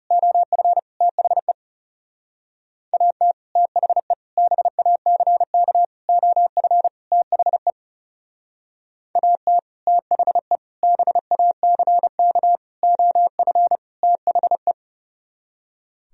Morse code, heard as a keyed tone, about 36 wpm, 700 Hz. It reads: OF THE AT THE BACK OF THE AT THE BACK OF THE
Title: Speed-Racing - Sets of 5 Words 30wpm